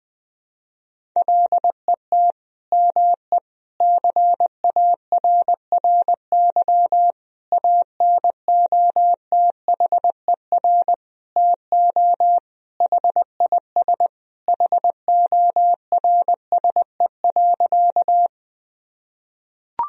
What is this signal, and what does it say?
Morse code, heard as a keyed tone, about 21 wpm, 700 Hz. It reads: LET ME CARRY ANOTHER TO HIS HORSE.
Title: Sentences from Top 1000 Words - Review Part 1 - 20wpm